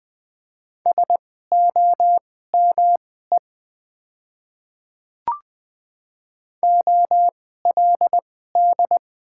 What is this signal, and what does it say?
Morse code, heard as a keyed tone, about 20 wpm, spaced as for 15 wpm, 700 Hz. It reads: SOME OLD
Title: 2x Character Spacing - Top 200 Words - 20wpm